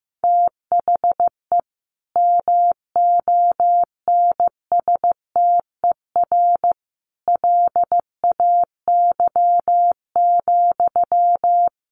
Morse code, heard as a keyed tone, about 15 wpm, 700 Hz. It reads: THE MONSTER LAY,
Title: A Sound Of Thunder 15wpm